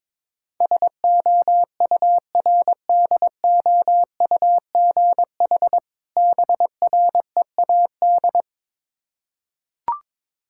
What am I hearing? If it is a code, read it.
Morse code, heard as a keyed tone, about 22 wpm, 690 Hz. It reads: SOURDOUGH BREAD
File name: Difficult Words with Hints - 22wpm